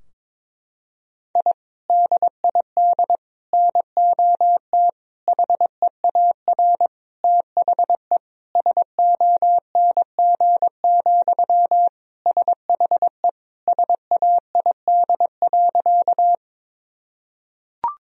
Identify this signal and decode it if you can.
Morse code, heard as a keyed tone, about 22 wpm, 700 Hz. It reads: I DID NOT HEAR THE SONG, SHE SAID.
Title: Sentences from Top 1000 Words - Review Part 2 - 22wpm